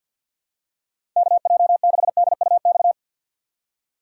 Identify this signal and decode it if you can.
Morse code, heard as a keyed tone, about 47 wpm, 690 Hz. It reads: KJ6DUX
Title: USA Call Signs - 6x Character Spacing - 50wpm